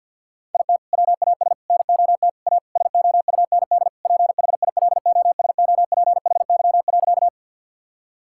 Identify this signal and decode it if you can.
Morse code, heard as a keyed tone, about 51 wpm, 700 Hz. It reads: IT WAS NOT A SOUND PHILOSOPHY.
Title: Sentences from Top 1000 Words - Review Part 6 - 50wpm